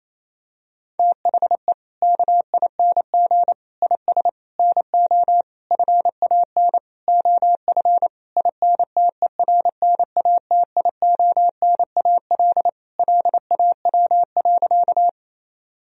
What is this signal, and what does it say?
Morse code, heard as a keyed tone, about 28 wpm, 700 Hz. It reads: THE KING IS NO FAN OF INTERNATIONAL LAW.
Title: Sentences from Top 1000 Words - Review Part 4 - 28wpm